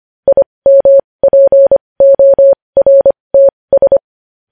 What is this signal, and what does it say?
Morse code, heard as a keyed tone, about 25 wpm, spaced as for 20 wpm, 550 Hz. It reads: IMPORTS
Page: News Headlines in Morse Code at 20 WPM